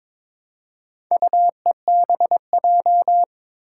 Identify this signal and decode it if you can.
Morse code, heard as a keyed tone, about 22 wpm, 700 Hz. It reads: UEBJ
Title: Four Challenging Letter-Number Combinations 22wpm